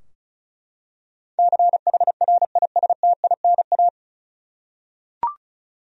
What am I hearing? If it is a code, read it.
Morse code, heard as a keyed tone, about 36 wpm, 700 Hz. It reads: CHRISTINA
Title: USA Common Names - 35wpm